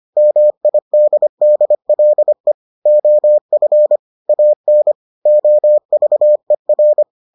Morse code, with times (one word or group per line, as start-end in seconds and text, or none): 0.2-2.5 MIDDLE
2.9-4.0 OF
4.3-4.9 AN
5.3-7.0 OVER